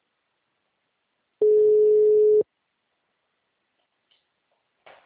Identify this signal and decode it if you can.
Ringback tone (Europe, 425 Hz).